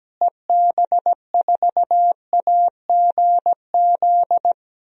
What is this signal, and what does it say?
Morse code, heard as a keyed tone, about 17 wpm, 700 Hz. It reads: EB4AGZ
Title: DXCC-100 Call Signs - 6 Characters - Repeated 3x - 17wpm